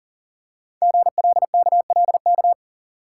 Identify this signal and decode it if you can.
Morse code, heard as a keyed tone, about 39 wpm, 700 Hz. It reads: GLKLK
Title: Code Groups - 5 Letters - 40wpm